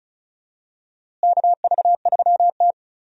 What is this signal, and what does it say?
Morse code, heard as a keyed tone, about 36 wpm, 700 Hz. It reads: KV3T
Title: USA Call Signs - 4 Characters - Repeated 2x - 35wpm